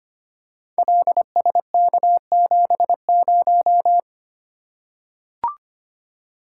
Morse code, heard as a keyed tone, about 25 wpm, 700 Hz. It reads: LSK70
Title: Code Groups - 5 Letter:Numbers - 25wpm